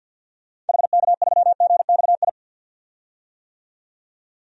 Morse code, heard as a keyed tone, about 47 wpm, 700 Hz. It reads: HK3GXI
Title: DXCC-100 Call Signs - 6 Characters - 50wpm